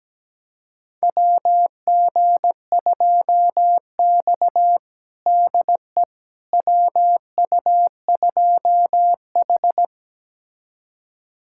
Morse code, heard as a keyed tone, about 17 wpm, 700 Hz. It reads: WG2X DE WU2H
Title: USA Callsign DE Callsign - 4 Characters - 17wpm